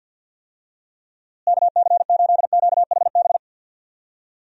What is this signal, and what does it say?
Morse code, heard as a keyed tone, about 51 wpm, 700 Hz. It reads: KC8QHB
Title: USA Call Signs - 6 Characters - Repeated 3x - 50wpm